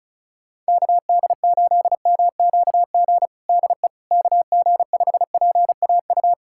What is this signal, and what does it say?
Morse code, heard as a keyed tone, about 36 wpm, 700 Hz. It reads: KD8MQG DE KG5PAU
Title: USA Callsign DE Callsign 35wpm